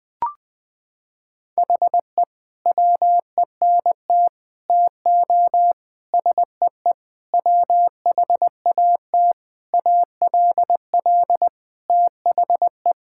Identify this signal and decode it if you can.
Morse code, heard as a keyed tone, about 20 wpm, 700 Hz. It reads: HE WENT TO SEE WHAT ALL THE
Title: Sentences with 8-Word Phrases - 20wpm